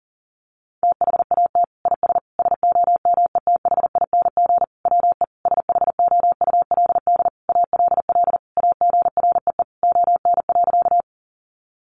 Morse code, heard as a keyed tone, about 40 wpm, 700 Hz. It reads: THAT IS SOMETHING WE SHOULD ALL AGREE ON.
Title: Sentences from Top 1100 Words 40wpm